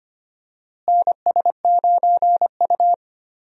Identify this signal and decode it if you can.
Morse code, heard as a keyed tone, about 25 wpm, 700 Hz. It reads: NS9U
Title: USA Call Signs - 4 Characters - Repeated 2x - 25wpm